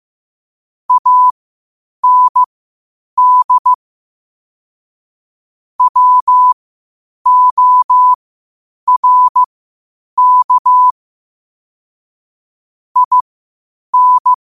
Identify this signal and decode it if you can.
Morse code, heard as a keyed tone, about 15 wpm, spaced as for 8 wpm, 1000 Hz. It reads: AND WORK IN
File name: qotdfiles / 8wpm